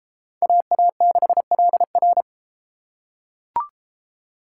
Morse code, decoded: AA6LR

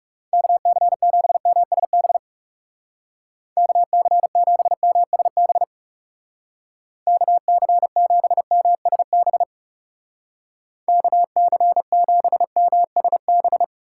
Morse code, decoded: KC7MSB KC7MSB KC7MSB KC7MSB

